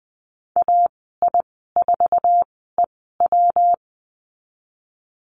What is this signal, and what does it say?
Morse code, heard as a keyed tone, about 19 wpm, spaced as for 14 wpm, 700 Hz. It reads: AI4EW
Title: 128 Lesson 21g - Number 6 - Call Signs 20wpm